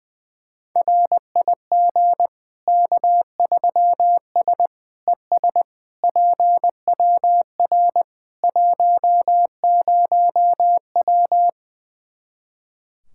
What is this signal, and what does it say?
Morse code, heard as a keyed tone, about 20 wpm, 700 Hz. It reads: RIG K3S ES PWR 10W
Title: Practice QSO - Exchange 2 - Info - 20wpm